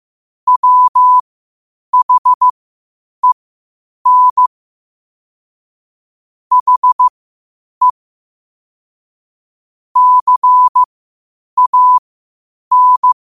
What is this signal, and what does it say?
Morse code, heard as a keyed tone, about 15 wpm, spaced as for 8 wpm, 1000 Hz. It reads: WHEN HE CAN